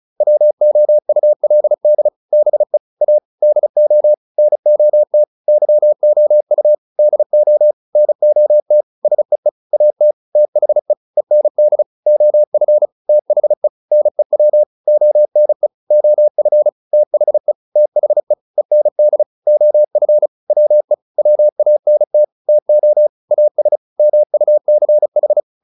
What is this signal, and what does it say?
Morse code, heard as a keyed tone, about 34 wpm, 600 Hz. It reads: WOULD BE A DO NOT YOU DO NOT SEE AT THE END OF THE NEW ONE OF THE THE END OF WE WANT TO AS MUCH